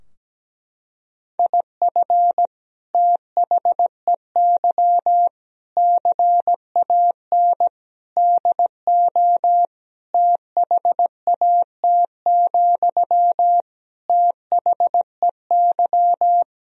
Morse code, heard as a keyed tone, about 17 wpm, 700 Hz. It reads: IF THEY CAN DO THAT, THEY
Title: Sentences from Top 1000 Words - Review Part 4 - 17wpm